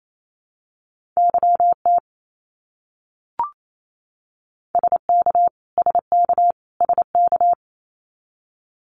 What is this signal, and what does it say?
Morse code, heard as a keyed tone, about 28 wpm, 700 Hz. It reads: YT SK SK SK
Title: Canadian Provinces and Territories - Repeated 3x - 28wpm